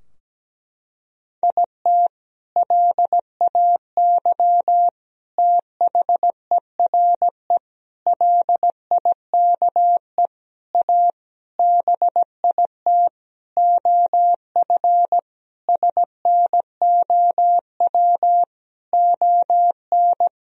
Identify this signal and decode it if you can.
Morse code, heard as a keyed tone, about 17 wpm, 700 Hz. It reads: IT LAY THERE LIKE A BIT OF SNOW ON
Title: Sentences from Top 1000 Words - Review Part 3 - 17wpm